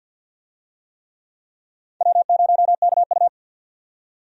Morse code, heard as a keyed tone, about 53 wpm, 700 Hz. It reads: W0KU
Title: USA Call Signs - 4 Characters - 50wpm